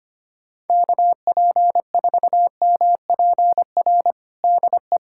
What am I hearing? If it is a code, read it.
Morse code, heard as a keyed tone, about 25 wpm, 700 Hz. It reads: KP4MPR DE
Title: USA Callsign DE Callsign - 6 Characters - 25wpm